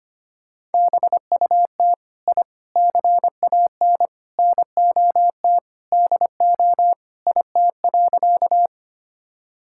Morse code, heard as a keyed tone, about 25 wpm, 700 Hz. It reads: BUT I CAN NOT DO IT.
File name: practice / 25wpm / Sentences with 4-Word Phrases - 25wpm